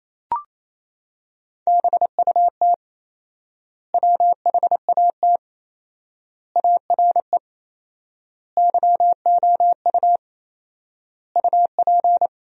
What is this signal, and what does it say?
Morse code, heard as a keyed tone, about 28 wpm, 700 Hz. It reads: BUT WHAT ARE YOU UP
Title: 4x Word-Spacing - Sentences from Top 100 Words - 28wpm